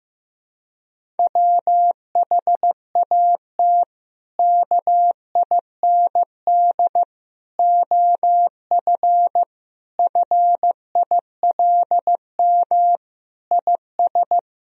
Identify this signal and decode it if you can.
Morse code, heard as a keyed tone, about 15 wpm, 700 Hz. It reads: WHAT KIND OF FILM IS
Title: Sentences from Top 1000 Words - Review Part 3 - 15wpm